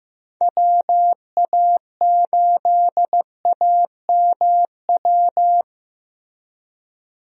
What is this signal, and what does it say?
Morse code, heard as a keyed tone, about 15 wpm, 700 Hz. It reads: WA8AMW
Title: USA Call Signs - 6 Characters - Repeated 3x - 15wpm